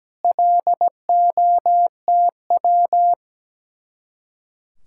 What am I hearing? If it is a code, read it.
Morse code, heard as a keyed tone, about 17 wpm, 700 Hz. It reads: LOTW
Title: Speed-Racing Common QSO Phrases - 17wpm